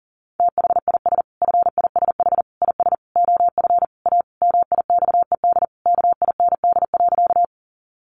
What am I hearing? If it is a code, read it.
Morse code, heard as a keyed tone, about 40 wpm, 700 Hz. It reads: THIS FISH IS OF A MIXED KIND.
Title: Sentences from Top 1500 Words 40wpm